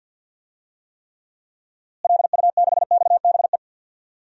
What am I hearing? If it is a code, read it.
Morse code, heard as a keyed tone, about 53 wpm, 700 Hz. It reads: LU6XBE
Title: DXCC-100 Call Signs - 6 Characters - Repeated 2x - 50wpm